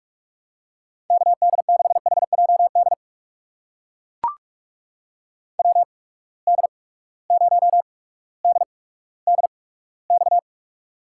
Morse code, heard as a keyed tone, about 42 wpm, spaced as for 19 wpm, 700 Hz. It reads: KD6HJD WD0DDX
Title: USA Call Signs - 6 Characters - 8x Character Spacing - 45wpm